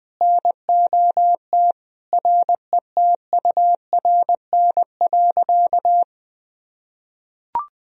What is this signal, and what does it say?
Morse code, heard as a keyed tone, about 20 wpm, 700 Hz. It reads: NOT RETURN.
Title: Sentences from Top 1000 Words - Review Part 1 - 20wpm